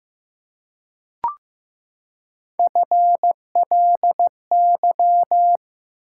Morse code, heard as a keyed tone, about 15 wpm, 700 Hz. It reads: FLY
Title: Three Letter Words - 15wpm